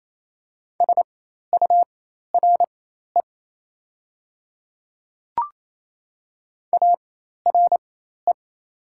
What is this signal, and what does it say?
Morse code, heard as a keyed tone, about 28 wpm, spaced as for 13 wpm, 700 Hz. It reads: SURE ARE
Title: 4x Character Spacing - Top 300 Words - 28wpm